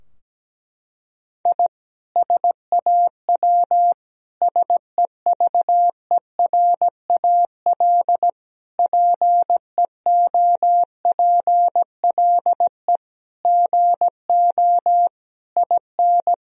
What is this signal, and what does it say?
Morse code, heard as a keyed tone, about 17 wpm, 700 Hz. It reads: I SAW SEVERAL PEOPLE GO IN
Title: Sentences from Top 1000 Words - Review Part 2 - 17wpm